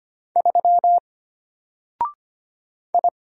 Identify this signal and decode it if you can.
Morse code, heard as a keyed tone, about 25 wpm, 700 Hz. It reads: 3 I
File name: Single-Letter-Number-25wpm